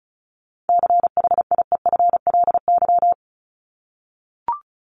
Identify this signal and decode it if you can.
Morse code, heard as a keyed tone, about 34 wpm, 700 Hz. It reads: CHIEFLY
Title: LY Suffix - Top 100 Words with Base Words 35wpm